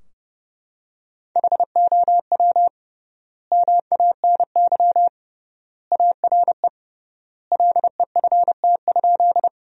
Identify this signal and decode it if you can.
Morse code, heard as a keyed tone, about 30 wpm, 700 Hz. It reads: HOW MANY ARE LEFT?
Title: 3x Word-Spacing - Sentences from Top 300 Words - 30wpm